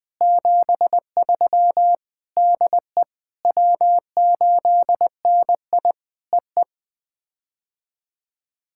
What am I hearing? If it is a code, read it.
Morse code, heard as a keyed tone, about 20 wpm, 700 Hz. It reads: 73 DE W8NI EE
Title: POTA QSO Practice - 20wpm